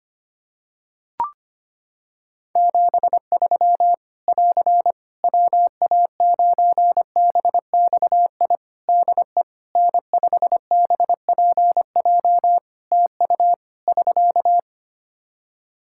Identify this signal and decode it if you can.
Morse code, heard as a keyed tone, about 25 wpm, 700 Hz. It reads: 73 <AR> WA9BXI DE N5BPJ TU <SK>
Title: K1USN SST QSO Practice 25wpm